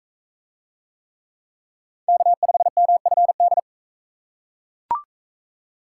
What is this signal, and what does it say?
Morse code, heard as a keyed tone, about 40 wpm, 700 Hz. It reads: K5MFD